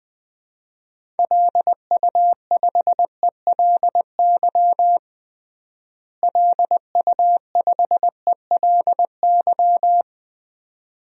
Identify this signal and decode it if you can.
Morse code, heard as a keyed tone, about 21 wpm, 700 Hz. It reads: LU5ELY LU5ELY
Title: DXCC-100 Call Signs - 6 Characters - Repeated 2x - 20wpm